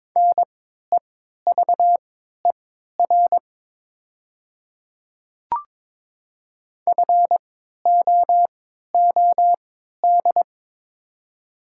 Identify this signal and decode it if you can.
Morse code, heard as a keyed tone, about 22 wpm, spaced as for 13 wpm, 700 Hz. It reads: NEVER FOOD
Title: 3x Character Spacing - Top 500 Words - 22wpm